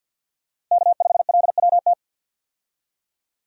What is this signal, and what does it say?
Morse code, heard as a keyed tone, about 46 wpm, 700 Hz. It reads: K5LWT
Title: USA Call Signs - 5 Characters - Repeated 3x - 50wpm